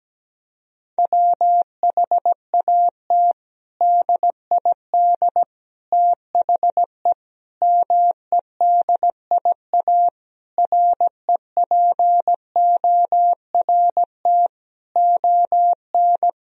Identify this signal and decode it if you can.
Morse code, heard as a keyed tone, about 17 wpm, 700 Hz. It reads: WHAT DID THE MEDIA REPORT ON
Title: Sentences from Top 1000 Words - Review Part 4 - 17wpm